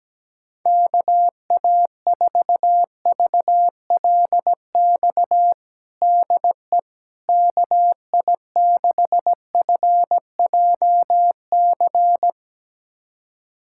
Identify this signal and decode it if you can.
Morse code, heard as a keyed tone, about 17 wpm, 700 Hz. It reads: KA4VLX DE KI6FJC
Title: USA Callsign DE Callsign - 6 Characters - 17wpm